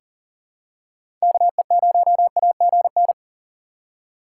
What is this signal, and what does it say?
Morse code, heard as a keyed tone, about 40 wpm, 700 Hz. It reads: KE0AGN
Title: USA Call Signs - 4x Character Spacing - 40wpm